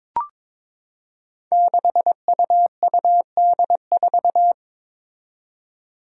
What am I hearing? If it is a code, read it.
Morse code, heard as a keyed tone, about 22 wpm, 700 Hz. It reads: 6UUD4
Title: Five Challenging Letter-Number Combinations 22wpm